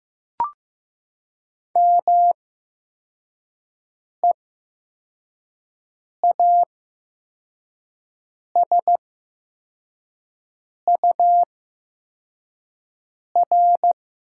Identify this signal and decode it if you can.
Morse code, heard as a keyed tone, about 15 wpm, spaced as for 4 wpm, 700 Hz. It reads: MEASUR
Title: 8x Character Spacing - Top 500 Words - 15wpm